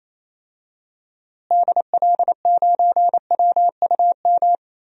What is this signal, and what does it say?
Morse code, heard as a keyed tone, about 28 wpm, 700 Hz. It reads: DL9WUM
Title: DXCC-100 Call Signs - 6 Characters - Repeated 2x - 28wpm